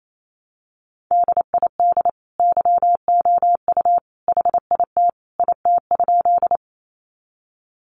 Morse code, heard as a keyed tone, about 28 wpm, 700 Hz. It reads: DID YOU HIT IT?
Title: Sentences from Top 1200 Words 28wpm